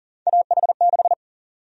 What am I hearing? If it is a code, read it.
Morse code, heard as a keyed tone, about 40 wpm, 700 Hz. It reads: AH6
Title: USA Call Sign Prefixes - 40wpm